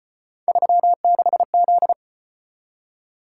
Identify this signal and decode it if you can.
Morse code, heard as a keyed tone, about 33 wpm, 700 Hz. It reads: 36Z